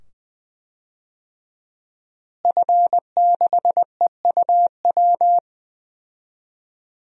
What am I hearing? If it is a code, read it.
Morse code, heard as a keyed tone, about 20 wpm, 700 Hz. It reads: F6EUW
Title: DXCC-100 Call Signs - 5 Characters - Repeated 2x - 20wpm